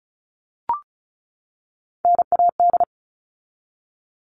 Morse code, heard as a keyed tone, about 34 wpm, 700 Hz. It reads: NAD
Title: All Cut Numbers - 3-Digit 35wpm